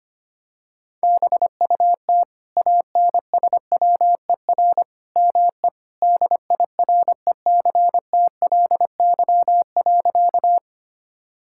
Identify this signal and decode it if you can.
Morse code, heard as a keyed tone, about 25 wpm, 700 Hz. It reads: BUT ANSWER ME DIRECTLY.
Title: Sentences from Top 1000 Words - Review Part 5 - 25wpm